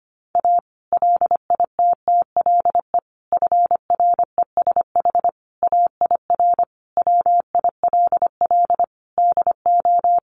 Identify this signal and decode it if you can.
Morse code, heard as a keyed tone, about 25 wpm, 700 Hz. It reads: A LITTLE FRESH AIR WILL DO
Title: Sentences from Top 1200 Words 25wpm